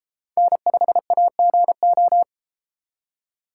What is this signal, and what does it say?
Morse code, heard as a keyed tone, about 32 wpm, 700 Hz. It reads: N5AGO